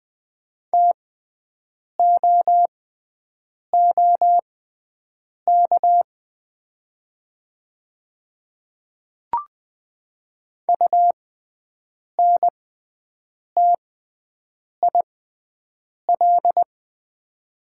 Morse code, heard as a keyed tone, about 20 wpm, spaced as for 7 wpm, 700 Hz. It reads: TOOK UNTIL
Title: 6x Character Spacing - Top 500 Words - 20wpm